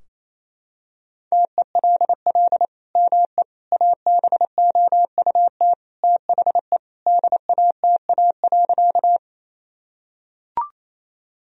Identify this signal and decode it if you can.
Morse code, heard as a keyed tone, about 28 wpm, 700 Hz. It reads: TELL ME ABOUT THE DATA.
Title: Sentences from Top 1000 Words - Review Part 3 - 28wpm